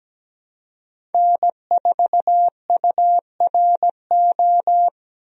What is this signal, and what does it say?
Morse code, heard as a keyed tone, about 17 wpm, 700 Hz. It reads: N4URO
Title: USA Call Signs - 5 Characters - 17wpm